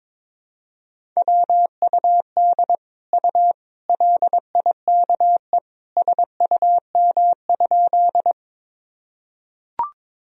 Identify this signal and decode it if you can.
Morse code, heard as a keyed tone, about 22 wpm, 700 Hz. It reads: WUD U LIKE SUM?
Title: Sentences from Top 100 Words and Abbreviations 22wpm